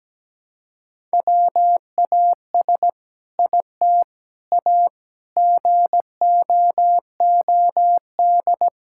Morse code, heard as a keyed tone, about 17 wpm, 700 Hz. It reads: WAS IT A GOOD
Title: Sentences from Top 1000 Words - Review Part 5 - 17wpm